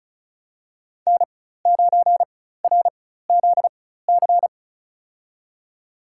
Morse code, heard as a keyed tone, about 34 wpm, spaced as for 16 wpm, 700 Hz. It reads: N9RZC
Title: USA Call Signs - 5 Characters - 4x Character Spacing - 35wpm